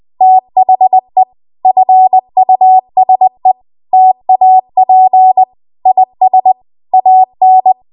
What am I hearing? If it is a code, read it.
Morse code, heard as a keyed tone, about 20 wpm, 750 Hz. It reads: THE FUSE TAP IS AN
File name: Archive / 20 WPM / 20WPM